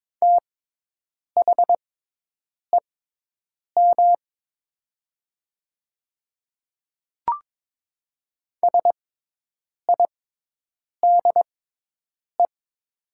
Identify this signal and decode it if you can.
Morse code, heard as a keyed tone, about 22 wpm, spaced as for 8 wpm, 700 Hz. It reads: THEM SIDE
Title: 6x Character Spacing - Top 300 Words - 22wpm